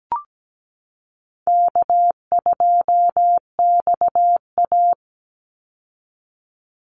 Morse code, heard as a keyed tone, about 17 wpm, 700 Hz. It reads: K2XA
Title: Easy Call Signs Repeated 2x - 17wpm